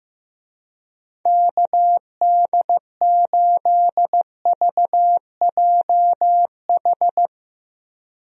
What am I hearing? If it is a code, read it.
Morse code, heard as a keyed tone, about 15 wpm, 700 Hz. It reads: KD8VJH
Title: USA Call Signs - 6 Characters - Repeated 2x - 15wpm